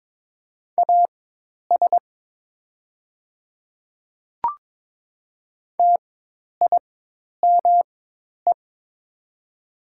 Morse code, heard as a keyed tone, about 22 wpm, spaced as for 10 wpm, 700 Hz. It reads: AS TIME
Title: 4x Character Spacing - Top 100 Words - 22wpm